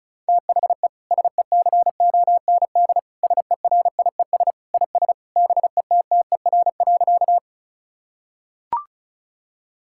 Morse code, heard as a keyed tone, about 36 wpm, 700 Hz. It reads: THE SECOND SERIES IS BETTER.
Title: Sentences from Top 1000 Words - Review Part 5 - 35wpm